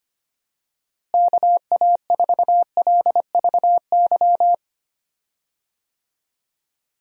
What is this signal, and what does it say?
Morse code, heard as a keyed tone, about 25 wpm, 700 Hz. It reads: KA4LVY
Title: USA Call Signs - 6 Characters - Repeated 2x - 25wpm